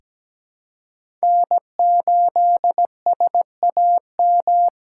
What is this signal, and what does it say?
Morse code, heard as a keyed tone, about 17 wpm, 700 Hz. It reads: N8SAM
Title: USA Call Signs - 5 Characters - 3x Character Spacing - 17wpm